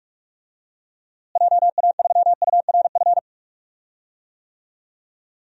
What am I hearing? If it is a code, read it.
Morse code, heard as a keyed tone, about 46 wpm, 700 Hz. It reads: JA3URF
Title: DXCC-100 Call Signs - 6 Characters - Repeated 2x - 45wpm